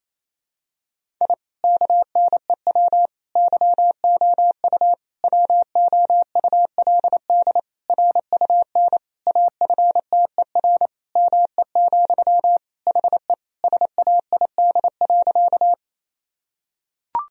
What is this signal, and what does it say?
Morse code, heard as a keyed tone, about 28 wpm, 700 Hz. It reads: I KNEW YOU WOULD RUN AFTER ME, HE SAID.
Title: Sentences from Top 1000 Words - Review Part 1 - 28wpm